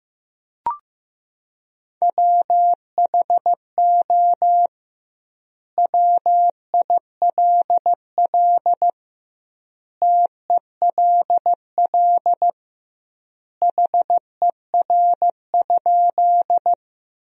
Morse code, heard as a keyed tone, about 15 wpm, 700 Hz. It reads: WHO WILL TELL HER?
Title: 2x Word-Spacing - Sentences from Top 200 Words - 15wpm